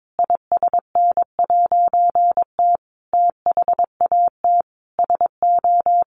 Morse code, heard as a keyed tone, about 22 wpm, 700 Hz. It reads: ISN'T THAT SO
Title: A Sound Of Thunder 22wpm